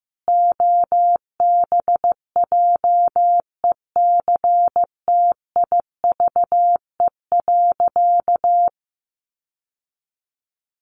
Morse code, heard as a keyed tone, about 15 wpm, 700 Hz. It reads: OBJECTIVE.
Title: Sentences from Top 1200 Words 15wpm